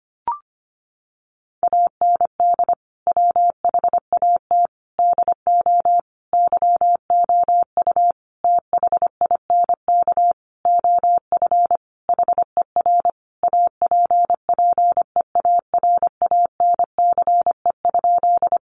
Morse code, heard as a keyed tone, about 25 wpm, 700 Hz. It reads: AND WHAT DO YOU THINK OF HER APPEARANCE?
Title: Sentences from Top 1100 Words 25wpm